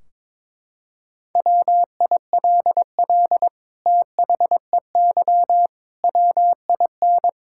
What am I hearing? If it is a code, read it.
Morse code, heard as a keyed tone, about 22 wpm, 700 Hz. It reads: WILL THEY WIN
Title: Sentences from Top 1600 Words 22wpm